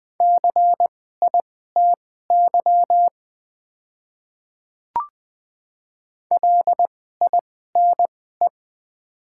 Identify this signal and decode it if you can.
Morse code, heard as a keyed tone, about 20 wpm, spaced as for 14 wpm, 700 Hz. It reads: CITY LINE
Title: 2x Character Spacing - Top 300 Words - 20wpm